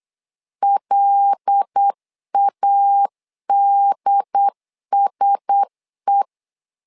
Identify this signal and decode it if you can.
Morse code, heard as a keyed tone, about 8 wpm, 800 Hz. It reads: LADSE
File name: LEZIONI / 5 LBVC / 5 LB 8wpm 2816